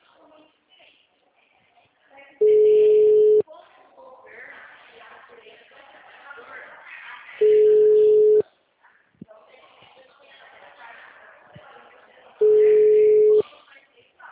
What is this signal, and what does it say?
Ringback tone (Europe, 425 Hz), with speech.
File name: out- -2048- - - .46604